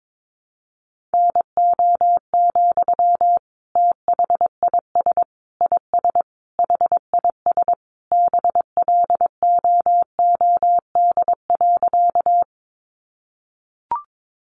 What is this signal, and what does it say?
Morse code, heard as a keyed tone, about 22 wpm, 700 Hz. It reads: NO, THIS IS HIS BLOOD.
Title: Sentences from Top 1100 Words 22wpm